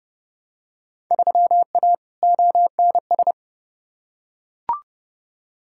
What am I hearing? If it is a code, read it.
Morse code, heard as a keyed tone, about 32 wpm, 700 Hz. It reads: 3A ONS
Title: Field Day Exchanges - 30wpm